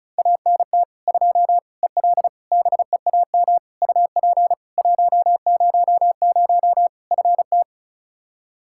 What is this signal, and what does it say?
Morse code, heard as a keyed tone, about 34 wpm, 700 Hz. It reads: ANT 2 EL BEAM UP 100 FT
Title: Practice QSO - Exchange 2 - Info - 35wpm